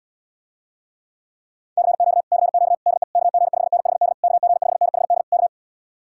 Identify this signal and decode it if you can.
Morse code, heard as a keyed tone, about 79 wpm, 700 Hz. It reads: CQ CQ DE KG5NHD KG5NHD K